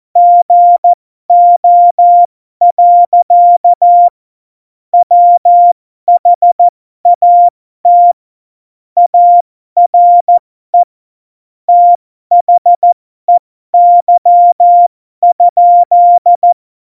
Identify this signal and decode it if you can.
Morse code, heard as a keyed tone, about 14 wpm, spaced as for 12 wpm, 700 Hz. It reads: GO. WHAT ARE THEY?
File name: Morsekurs / Witze